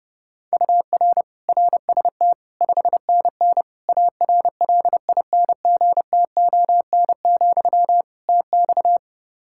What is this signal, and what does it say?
Morse code, heard as a keyed tone, about 32 wpm, 700 Hz. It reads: UR RST 5NN ARLINGTON, TX
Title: Practice QSO - Exchange 1 - Beginning - 30wpm